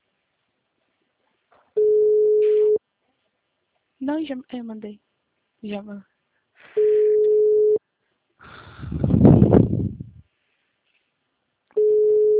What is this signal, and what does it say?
Ringback tone (Europe, 425 Hz), with speech.